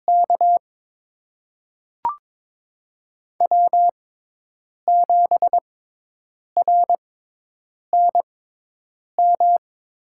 Morse code, heard as a keyed tone, about 22 wpm, spaced as for 8 wpm, 700 Hz. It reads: K W7RNM
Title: USA Call Signs - 5 Characters - 6x Character Spacing - 22wpm